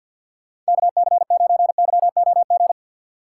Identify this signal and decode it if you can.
Morse code, heard as a keyed tone, about 52 wpm, 700 Hz. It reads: KC9YOG
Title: USA Call Signs - 6 Characters - 3x Character Spacing - 50wpm